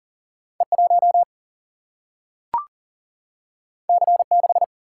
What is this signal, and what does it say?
Morse code, heard as a keyed tone, about 40 wpm, 700 Hz. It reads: E1 C6
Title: Single-Letter Single-Number 40wpm